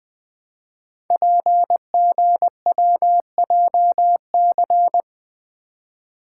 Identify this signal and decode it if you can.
Morse code, heard as a keyed tone, about 20 wpm, 700 Hz. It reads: PGWJC
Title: Code Groups - 5 Letters - 20wpm